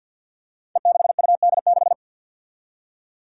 Morse code, heard as a keyed tone, about 52 wpm, 700 Hz. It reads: E6UD6
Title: Five Challenging Letter-Number Combinations 50wpm